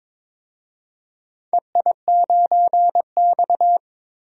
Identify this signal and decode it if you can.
Morse code, heard as a keyed tone, about 22 wpm, 700 Hz. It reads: EI9X
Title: DXCC-100 Call Signs - 4 Characters - 22wpm